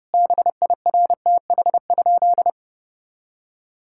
Morse code, heard as a keyed tone, about 32 wpm, 700 Hz. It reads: BIRTH?
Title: Sentences from Top 1000 Words - Review Part 5 - 30wpm